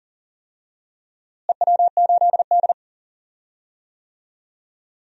Morse code, heard as a keyed tone, about 40 wpm, 700 Hz. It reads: EW8D
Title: DXCC-100 Call Signs - 4 Characters - Repeated 2x - 40wpm